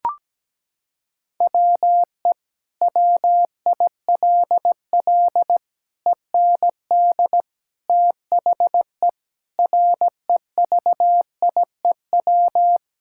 Morse code, heard as a keyed tone, about 17 wpm, 700 Hz. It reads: WE WILL END THE REVIEW
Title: Sentences from Top 1000 Words - Review Part 6 - 17wpm